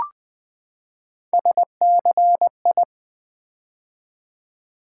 Morse code, heard as a keyed tone, about 20 wpm, 700 Hz. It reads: SCI
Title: Build A Word - 3 Letters At A Time - 9-Letter Words - 20wpm